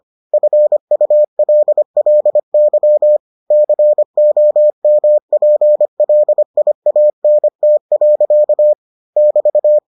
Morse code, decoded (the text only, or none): FULLY COMPLIANT. <BT>